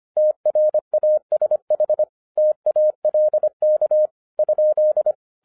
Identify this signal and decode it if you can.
Morse code, heard as a keyed tone, about 26 wpm, 610 Hz. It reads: TRASH TALK?
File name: 2012-01-mp3 / n14-FUN WITH NUMBERS-25wpm